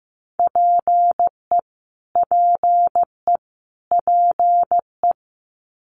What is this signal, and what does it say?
Morse code, heard as a keyed tone, about 15 wpm, 700 Hz. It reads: PE PE PE
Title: Canadian Provinces and Territories - Repeated 3x - 15wpm